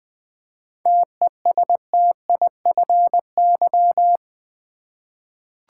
Morse code, heard as a keyed tone, about 20 wpm, 700 Hz. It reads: TESTIFY
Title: 7 Letter Words - First 3 Letters Given - 20wpm